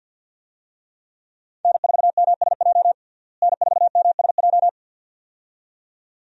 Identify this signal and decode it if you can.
Morse code, heard as a keyed tone, about 46 wpm, 700 Hz. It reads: N4MSJ N4MSJ
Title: USA Call Signs - 5 Characters - Repeated 2x - 50wpm